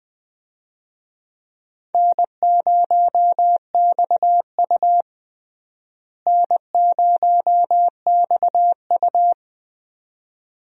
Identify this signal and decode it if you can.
Morse code, heard as a keyed tone, about 21 wpm, 700 Hz. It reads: N0XU N0XU
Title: USA Call Signs - 4 Characters - Repeated 2x - 20wpm